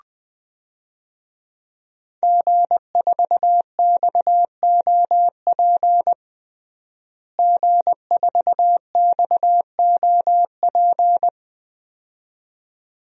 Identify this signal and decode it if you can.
Morse code, heard as a keyed tone, about 20 wpm, 700 Hz. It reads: G4XOP G4XOP
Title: DXCC-100 Call Signs - 5 Characters - Repeated 2x - 20wpm